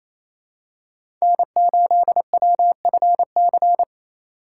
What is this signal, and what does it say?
Morse code, heard as a keyed tone, about 28 wpm, 700 Hz. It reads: N8WFC
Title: USA Call Signs - 2x Character Spacing - 28wpm